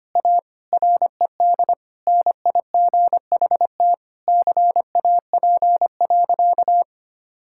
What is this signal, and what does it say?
Morse code, heard as a keyed tone, about 25 wpm, 700 Hz. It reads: A RED NIGHT CAP.
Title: Sentences from Top 1600 Words 25wpm